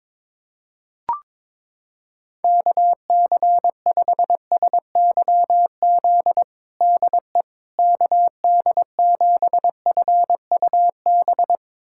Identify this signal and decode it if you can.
Morse code, heard as a keyed tone, about 22 wpm, 700 Hz. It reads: KC5SYZ DE KD7FUB
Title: USA Callsign DE Callsign - 6 Characters - 22wpm